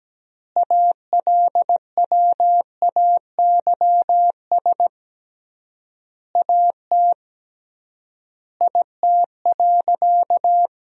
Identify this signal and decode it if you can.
Morse code, heard as a keyed tone, about 17 wpm, 700 Hz. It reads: ALWAYS AT IT.
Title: 3x Word-Spacing - Sentences from Top 300 Words - 17wpm